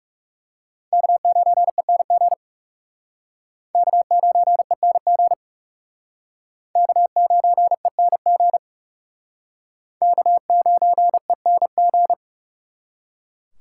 Morse code, heard as a keyed tone, about 39 wpm, 700 Hz. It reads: K9ENG K9ENG K9ENG K9ENG